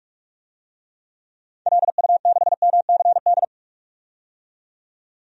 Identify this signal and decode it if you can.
Morse code, heard as a keyed tone, about 41 wpm, 700 Hz. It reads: LU6MCD
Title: DXCC-100 Call Signs - 6 Characters - 45wpm